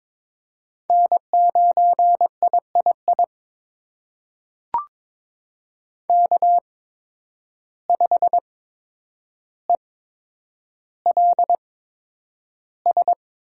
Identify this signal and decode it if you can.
Morse code, heard as a keyed tone, about 22 wpm, spaced as for 9 wpm, 700 Hz. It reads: N9III K5ELS
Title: USA Call Signs - 5 Characters - 8x Character Spacing - 22wpm